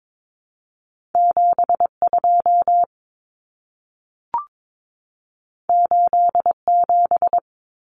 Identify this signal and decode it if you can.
Morse code, heard as a keyed tone, about 22 wpm, 700 Hz. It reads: 72 87
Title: Numbers - 2-Digit 22wpm